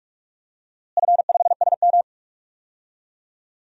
Morse code, heard as a keyed tone, about 46 wpm, 700 Hz. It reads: F5SM
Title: DXCC-100 Call Signs - 4 Characters - Repeated 3x - 45wpm